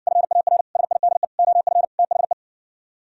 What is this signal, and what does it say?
Morse code, heard as a keyed tone, about 59 wpm, 700 Hz. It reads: FAR SIDE OF THE